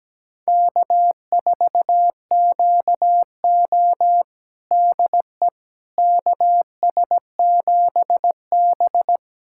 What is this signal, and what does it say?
Morse code, heard as a keyed tone, about 17 wpm, 700 Hz. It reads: K4QO DE KS7B
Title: USA Callsign DE Callsign - 4 Characters - 17wpm